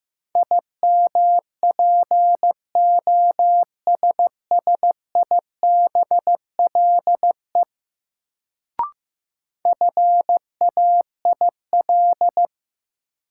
Morse code, heard as a keyed tone, about 15 wpm, 700 Hz. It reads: IMPOSSIBLE FAIL
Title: Sentences from Top 1100 Words - New Word Drill - 15wpm